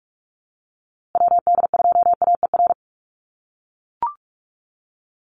Morse code, heard as a keyed tone, about 44 wpm, 700 Hz. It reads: WD2AER
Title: USA Call Signs Repeated 3x 45wpm